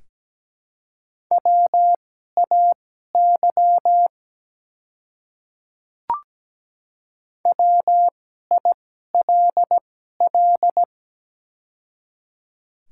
Morse code, heard as a keyed tone, about 17 wpm, spaced as for 12 wpm, 700 Hz. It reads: WAY WILL
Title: 2x Character Spacing - Top 100 Words - 17wpm